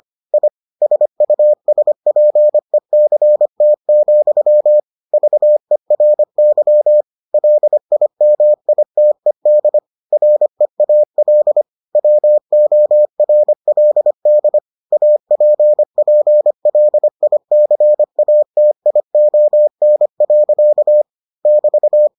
Morse code, heard as a keyed tone, about 25 wpm, 600 Hz. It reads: I SUSPECT, VERY LIMITED REAL WORLD APPLICATION. <BT>